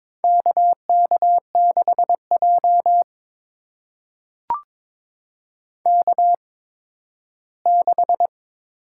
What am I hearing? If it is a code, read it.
Morse code, heard as a keyed tone, about 21 wpm, spaced as for 13 wpm, 700 Hz. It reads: KK6J K6
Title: USA Call Signs - 4 Characters - 8x Character Spacing - 22wpm